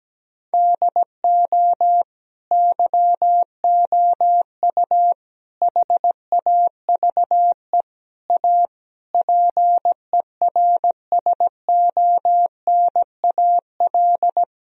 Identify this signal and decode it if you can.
Morse code, heard as a keyed tone, about 17 wpm, 700 Hz. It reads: DO YOU HAVE A PERSONAL
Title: Sentences from Top 1000 Words - Review Part 4 - 17wpm